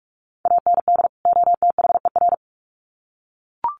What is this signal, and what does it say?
Morse code, heard as a keyed tone, about 44 wpm, 700 Hz. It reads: AND OTHER
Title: Sets of 2 Words 45wpm